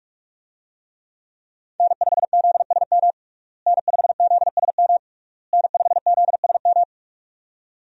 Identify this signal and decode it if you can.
Morse code, heard as a keyed tone, about 46 wpm, 700 Hz. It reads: N5ZSM N5ZSM N5ZSM
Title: USA Call Signs - 5 Characters - Repeated 3x - 45wpm